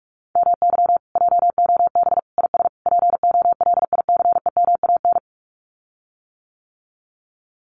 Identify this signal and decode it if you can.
Morse code, heard as a keyed tone, about 44 wpm, 700 Hz. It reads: MY JOB IS POLICEMAN
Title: Common QSO Phrases Encore 45wpm